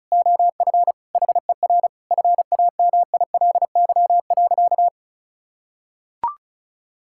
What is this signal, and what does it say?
Morse code, heard as a keyed tone, about 36 wpm, 700 Hz. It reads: OF HER FAMILY.
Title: Sentences from Top 1000 Words - Review Part 5 - 35wpm